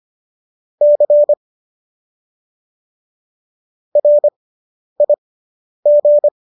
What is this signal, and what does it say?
Morse code, heard as a keyed tone, about 25 wpm, spaced as for 10 wpm, 590 Hz. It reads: C RIG